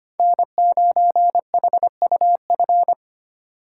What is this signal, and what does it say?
Morse code, heard as a keyed tone, about 25 wpm, 700 Hz. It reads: N9HUF
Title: USA Call Signs - 3x Character Spacing - 25wpm